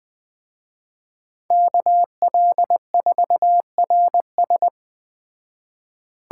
Morse code, heard as a keyed tone, about 20 wpm, 700 Hz. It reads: KL4RS
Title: Speed Racing - USA 2x2 Call Signs 17wpm